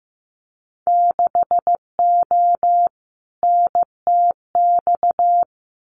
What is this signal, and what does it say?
Morse code, heard as a keyed tone, about 15 wpm, 700 Hz. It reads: 6O NTX
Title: Winter Field Day Exchanges Encore 15wpm